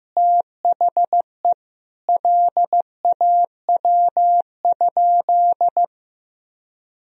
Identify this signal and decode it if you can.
Morse code, heard as a keyed tone, about 15 wpm, 700 Hz. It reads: THE LAW?
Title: Sentences from Top 1000 Words - Review Part 3 - 15wpm